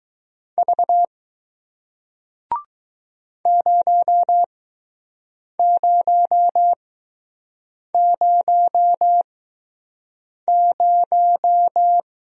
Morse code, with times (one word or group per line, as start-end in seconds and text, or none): 0.6-1.1 V
3.5-12.0 0000